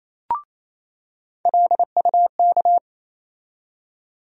Morse code, heard as a keyed tone, about 27 wpm, 700 Hz. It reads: LUK
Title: USA Call Sign Postfixes - 3 Letters 28wpm